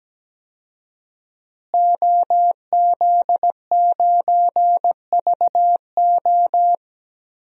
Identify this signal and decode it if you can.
Morse code, heard as a keyed tone, about 17 wpm, 700 Hz. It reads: OZ9VO
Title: DXCC-100 Call Signs - 5 Characters - Repeated 2x - 17wpm